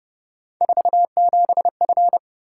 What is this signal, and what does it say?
Morse code, heard as a keyed tone, about 29 wpm, 700 Hz. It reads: 47F